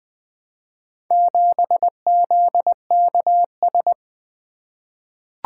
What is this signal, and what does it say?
Morse code, heard as a keyed tone, about 20 wpm, 700 Hz. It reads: 7ZKS
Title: Four Letter-Number Combinations - 20wpm